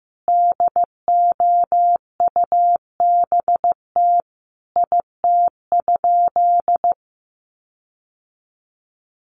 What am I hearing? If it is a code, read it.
Morse code, heard as a keyed tone, about 15 wpm, 700 Hz. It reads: DOUBT IT?
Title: Sentences from Top 1500 Words 15wpm